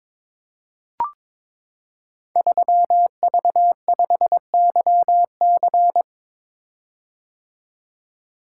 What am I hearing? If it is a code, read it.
Morse code, heard as a keyed tone, about 23 wpm, 700 Hz. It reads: 3V5YC
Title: Code Groups - 5 Letter:Numbers - 22wpm